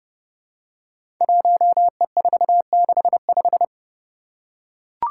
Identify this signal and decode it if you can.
Morse code, heard as a keyed tone, about 29 wpm, 700 Hz. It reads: 1E465
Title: Code Groups - 5 Letter:Numbers - 30wpm